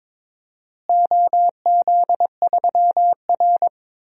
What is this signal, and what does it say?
Morse code, heard as a keyed tone, about 22 wpm, 700 Hz. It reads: OZ3R
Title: DXCC-100 Call Signs - 4 Characters - Repeated 3x - 22wpm